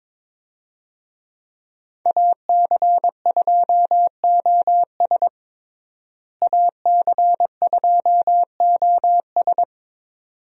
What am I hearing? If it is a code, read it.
Morse code, heard as a keyed tone, about 22 wpm, 700 Hz. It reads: AC2OS AC2OS
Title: USA Call Signs - 5 Characters - Repeated 2x - 22wpm